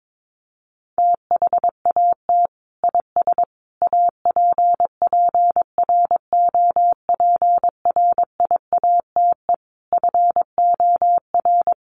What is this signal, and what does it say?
Morse code, heard as a keyed tone, about 22 wpm, 700 Hz. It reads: THAT IS APPROPRIATE FOR
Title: Sentences from Top 900 Words 22wpm